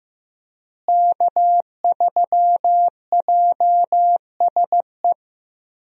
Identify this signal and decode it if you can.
Morse code, heard as a keyed tone, about 15 wpm, 700 Hz. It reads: K3JSE
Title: USA Call Signs - 5 Characters - Repeated 3x - 15wpm